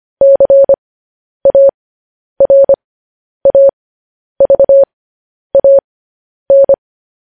Morse code, heard as a keyed tone, about 25 wpm, spaced as for 10 wpm, 550 Hz. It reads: CARAVAN
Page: news headlines in morse code at 10 wpm